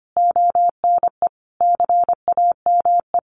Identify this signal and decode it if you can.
Morse code, heard as a keyed tone, about 25 wpm, 700 Hz. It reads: ONE CAME
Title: Sentences from Top 1300 Words 25wpm